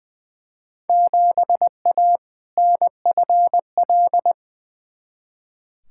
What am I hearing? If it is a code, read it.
Morse code, heard as a keyed tone, about 20 wpm, 700 Hz. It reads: 7A NFL
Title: Field Day Exchanges Encore - 20wpm